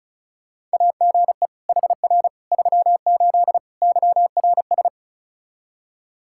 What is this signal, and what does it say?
Morse code, heard as a keyed tone, about 36 wpm, 700 Hz. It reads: AGE HR 38 YRS
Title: Practice QSO - Exchange 3 - More Info - 35wpm